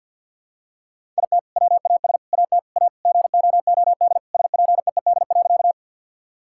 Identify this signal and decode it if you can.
Morse code, heard as a keyed tone, about 50 wpm, 700 Hz. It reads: IT WAS AT A GOOD SPEED.
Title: Sentences from Top 1000 Words - Review Part 5 - 50wpm